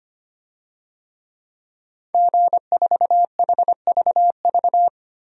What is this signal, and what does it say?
Morse code, heard as a keyed tone, about 25 wpm, 700 Hz. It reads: G4HVV
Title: DXCC-100 Call Signs - 5 Characters - Repeated 3x - 25wpm